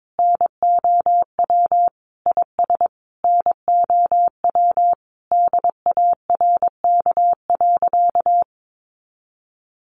Morse code, heard as a keyed tone, about 22 wpm, 700 Hz. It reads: NOW IS NOW DARK.
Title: Sentences from Top 700 Words 22wpm